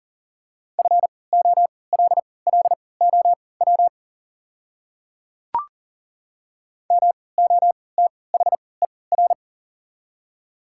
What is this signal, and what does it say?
Morse code, heard as a keyed tone, about 40 wpm, spaced as for 23 wpm, 700 Hz. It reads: FOLLOW MOTHER
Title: 3x Character Spacing - Top 200 Words - 40wpm